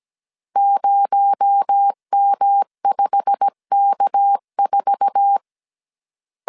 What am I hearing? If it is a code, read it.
Morse code, heard as a keyed tone, about 17 wpm, 800 Hz. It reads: 0M5K4
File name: CW / DET / 1 9 1500C S A 16wpm 2828